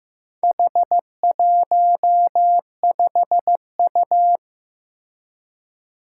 Morse code, heard as a keyed tone, about 15 wpm, 700 Hz. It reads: H15U
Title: Four Challenging Letter-Number Combinations 15wpm